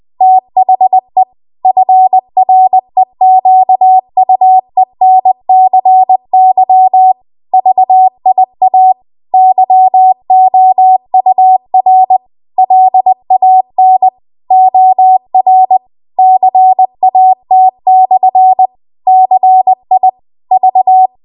Morse code, heard as a keyed tone, about 20 wpm, 750 Hz. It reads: THE FREQUENCY VIA YOUR LAN OR CAT/CIV
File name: Archive / 20 WPM / 20WPM